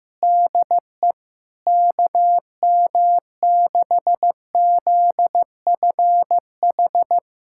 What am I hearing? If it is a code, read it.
Morse code, heard as a keyed tone, about 15 wpm, 700 Hz. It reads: DE KM6ZFH
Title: USA Callsign DE Callsign - 6 Characters - 15wpm